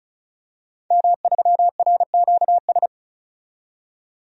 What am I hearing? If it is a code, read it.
Morse code, heard as a keyed tone, about 34 wpm, 700 Hz. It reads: M3RQS
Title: DXCC-100 Call Signs - 5 Characters - 35wpm